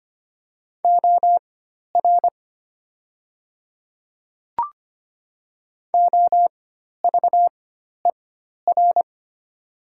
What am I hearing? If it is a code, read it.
Morse code, heard as a keyed tone, about 25 wpm, spaced as for 12 wpm, 690 Hz. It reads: OR OVER
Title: 4x Character Spacing - Top 200 Words - 25wpm